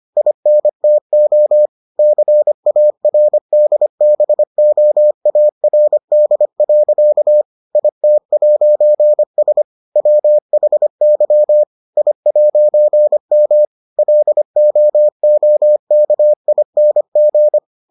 Morse code, heard as a keyed tone, about 25 wpm, 600 Hz. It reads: INTO CARDBOARD. IT'S WHY I'M LOOKING